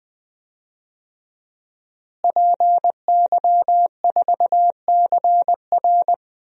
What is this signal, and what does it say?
Morse code, heard as a keyed tone, about 21 wpm, 700 Hz. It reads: PY4CR
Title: DXCC-100 Call Signs - 5 Characters - Repeated 2x - 20wpm